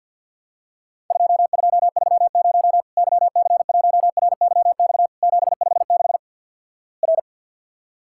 Morse code, heard as a keyed tone, about 51 wpm, 700 Hz. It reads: 2230 YC1RYX 756 R
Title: Worked All Europe DX Contest - QTC Exchange - 50wpm